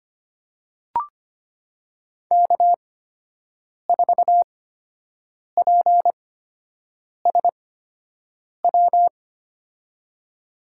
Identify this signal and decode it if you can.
Morse code, heard as a keyed tone, about 25 wpm, spaced as for 7 wpm, 700 Hz. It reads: K4PSW
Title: USA Call Signs - 5 Characters - 8x Character Spacing - 25wpm